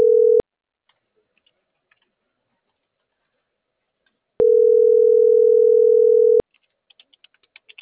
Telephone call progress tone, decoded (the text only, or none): ringback tone